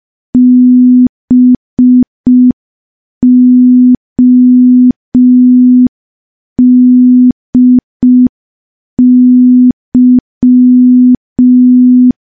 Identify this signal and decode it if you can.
Morse code, heard as a keyed tone, about 5 wpm, 250 Hz. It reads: BODY
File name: morse / morse-training-05WPM